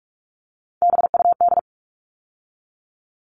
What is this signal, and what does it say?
Morse code, heard as a keyed tone, about 44 wpm, 700 Hz. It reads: BUD
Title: All Cut Numbers - 3-Digit 45wpm